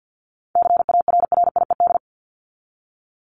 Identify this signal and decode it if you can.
Morse code, heard as a keyed tone, about 50 wpm, 700 Hz. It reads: CARRIED
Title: ED Suffix - Top 100 Words with Base Words 50wpm